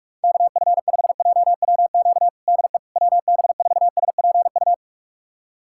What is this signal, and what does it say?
Morse code, heard as a keyed tone, about 46 wpm, 700 Hz. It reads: KF5JWQ DE WB4SPU
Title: USA Callsign DE Callsign - 6 Characters - 45wpm